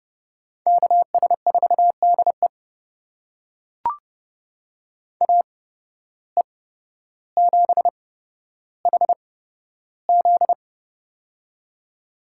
Morse code, heard as a keyed tone, about 31 wpm, spaced as for 13 wpm, 700 Hz. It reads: KS4DE AE7HZ
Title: USA Call Signs - 5 Characters - 8x Character Spacing - 30wpm